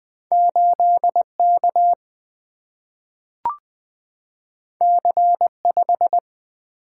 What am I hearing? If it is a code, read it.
Morse code, heard as a keyed tone, about 20 wpm, 700 Hz. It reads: 8K C5
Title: Single-Letter Single-Number 20wpm